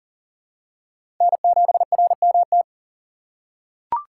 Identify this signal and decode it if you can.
Morse code, heard as a keyed tone, about 40 wpm, 700 Hz. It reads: N7RMT
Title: USA Call Signs - 5 Characters - 2x Character Spacing - 40wpm